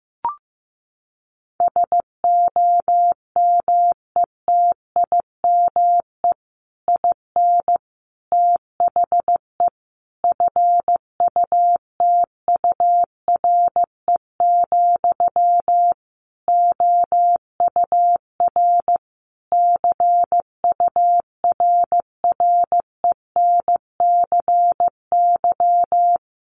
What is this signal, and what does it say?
Morse code, heard as a keyed tone, about 15 wpm, 700 Hz. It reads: SOMETIME IN THE FUTURE, OUR CURRENCY
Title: Sentences from Top 1500 Words 15wpm